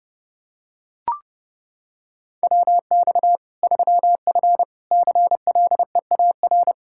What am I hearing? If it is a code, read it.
Morse code, heard as a keyed tone, about 32 wpm, 700 Hz. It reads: WX 3F CLEAR
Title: K1USN SST QSO Practice 30wpm